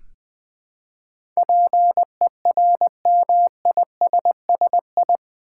Morse code, heard as a keyed tone, about 20 wpm, 700 Hz. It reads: PERMISSI
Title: Build A Word - 4 Letters At A Time - 10-Letter Words - 20wpm